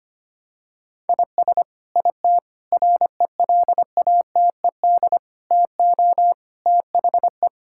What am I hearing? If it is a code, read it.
Morse code, heard as a keyed tone, about 25 wpm, 700 Hz. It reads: IS IT RELATED TO THE
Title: Sentences from Top 1000 Words - Review Part 5 - 25wpm